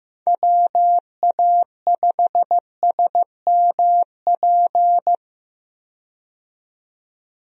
Morse code, heard as a keyed tone, about 15 wpm, 700 Hz. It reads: WA5SMP
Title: USA Call Signs - 6 Characters - 15wpm